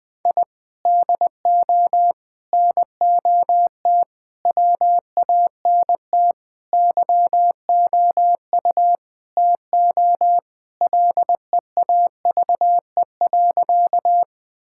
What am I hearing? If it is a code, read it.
Morse code, heard as a keyed tone, about 20 wpm, 700 Hz. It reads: I DO NOT WANT YOU TO LEAVE.
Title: Sentences with 5-Word Phrases - 20wpm